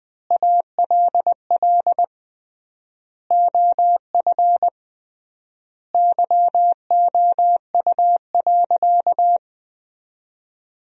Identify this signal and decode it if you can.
Morse code, heard as a keyed tone, about 21 wpm, 700 Hz. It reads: ALL OF YOU.
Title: 3x Word-Spacing - Sentences from Top 500 Words - 20wpm